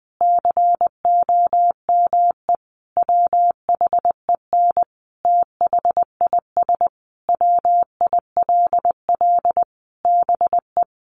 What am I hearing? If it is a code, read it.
Morse code, heard as a keyed tone, about 20 wpm, 700 Hz. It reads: COME WHEN THIS WILL BE
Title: Sentences from Top 1500 Words 20wpm